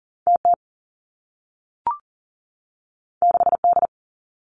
Morse code, heard as a keyed tone, about 42 wpm, 700 Hz. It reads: T T 6 D
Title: Two Challenging Letter-Number Combinations 40wpm